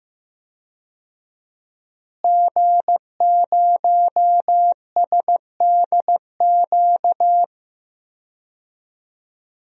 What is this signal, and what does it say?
Morse code, heard as a keyed tone, about 15 wpm, 700 Hz. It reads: G0SDQ
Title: DXCC-100 Call Signs - 5 Characters - Repeated 3x - 15wpm